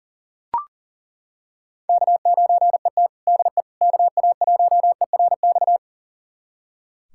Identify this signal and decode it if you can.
Morse code, heard as a keyed tone, about 40 wpm, 700 Hz. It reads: K9ET DE KA1ERX
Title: USA Callsign DE Callsign 40wpm